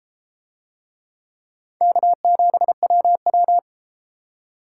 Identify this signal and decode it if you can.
Morse code, heard as a keyed tone, about 31 wpm, 700 Hz. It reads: K7WW